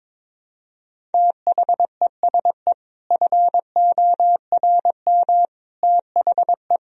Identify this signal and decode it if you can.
Morse code, heard as a keyed tone, about 22 wpm, 700 Hz. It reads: THESE FORM THE
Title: Sentences from Top 1000 Words - Review Part 1 - 22wpm